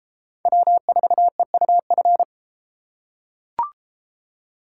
Morse code, heard as a keyed tone, about 32 wpm, 700 Hz. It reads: W4EUF